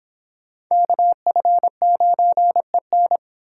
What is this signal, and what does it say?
Morse code, heard as a keyed tone, about 26 wpm, 700 Hz. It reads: KF9EN